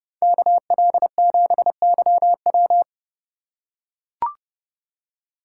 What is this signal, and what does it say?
Morse code, heard as a keyed tone, about 29 wpm, 700 Hz. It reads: KL7YW